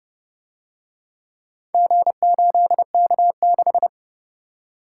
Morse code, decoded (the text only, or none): G8K6